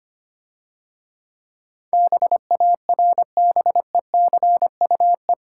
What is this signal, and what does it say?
Morse code, heard as a keyed tone, about 25 wpm, 700 Hz. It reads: BARBECUE
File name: practice / 25wpm / Difficult Words - Speed-Racing with Hints - 25wpm